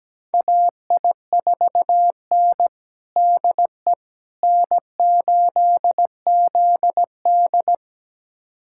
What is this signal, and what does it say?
Morse code, heard as a keyed tone, about 17 wpm, 700 Hz. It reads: AI4N DE N8ZD
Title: USA Callsign DE Callsign - 4 Characters - 17wpm